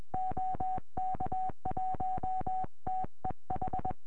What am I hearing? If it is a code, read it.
Morse code, heard as a keyed tone, about 21 wpm, 750 Hz. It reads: OK1TEH